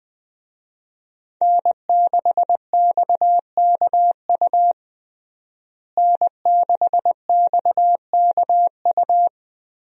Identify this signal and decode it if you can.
Morse code, heard as a keyed tone, about 21 wpm, 700 Hz. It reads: N6XKU N6XKU
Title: USA Call Signs - 5 Characters - Repeated 3x - 20wpm